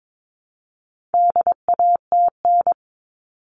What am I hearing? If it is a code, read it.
Morse code, heard as a keyed tone, about 22 wpm, 700 Hz. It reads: DATN
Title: All Cut Numbers - 4-Digit 22wpm